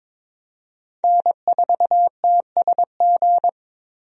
Morse code, heard as a keyed tone, about 21 wpm, 700 Hz. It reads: N4TSG
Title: USA Call Signs - 5 Characters - Repeated 3x - 22wpm